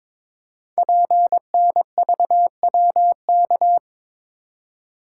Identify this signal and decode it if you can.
Morse code, heard as a keyed tone, about 22 wpm, 700 Hz. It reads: PNVWK
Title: Code Groups - 5 Letters - 22wpm